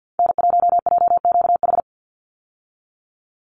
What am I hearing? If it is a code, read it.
Morse code, heard as a keyed tone, about 50 wpm, 700 Hz. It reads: N1JQH
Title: USA Call Signs 50wpm